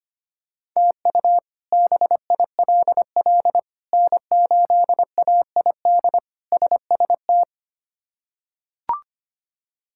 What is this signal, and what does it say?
Morse code, heard as a keyed tone, about 25 wpm, 700 Hz. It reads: TU BILL N8AID SST
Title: K1USN SST QSO Practice - 25wpm